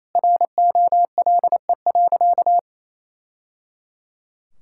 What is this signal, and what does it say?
Morse code, heard as a keyed tone, about 27 wpm, 700 Hz. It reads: ROLE.
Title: Sentences from Top 1000 Words - Review Part 6 - 28wpm